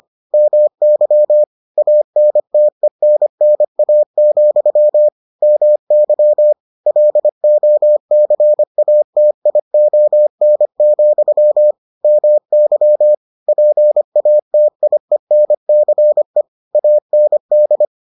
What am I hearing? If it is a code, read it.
Morse code, heard as a keyed tone, about 25 wpm, 600 Hz. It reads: MY ANTENNA, MY LOCATION, MY PATIENCE AND